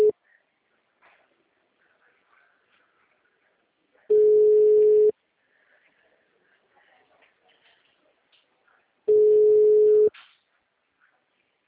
Ringback tone (Europe, 425 Hz).